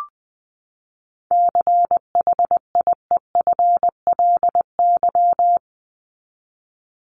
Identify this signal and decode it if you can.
Morse code, heard as a keyed tone, about 20 wpm, 700 Hz. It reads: CHIEFLY
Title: LY Suffix - Top 100 Words with Base Words 20wpm